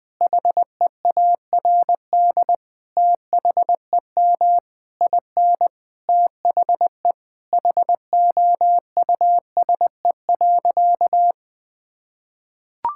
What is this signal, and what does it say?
Morse code, heard as a keyed tone, about 21 wpm, 700 Hz. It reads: HEARD THEM IN THE HOUSE.
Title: Sentences from Top 1000 Words - Review Part 5 - 20wpm